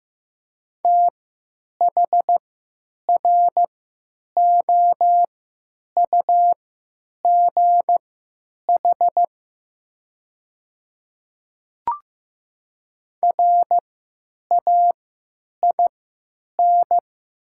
Morse code, heard as a keyed tone, about 15 wpm, spaced as for 9 wpm, 700 Hz. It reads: THROUGH RAIN
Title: 3x Character Spacing - Top 500 Words - 15wpm